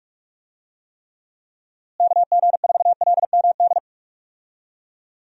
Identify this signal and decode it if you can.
Morse code, heard as a keyed tone, about 46 wpm, 700 Hz. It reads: KG4LMD
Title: USA Call Signs - 6 Characters - Repeated 2x - 45wpm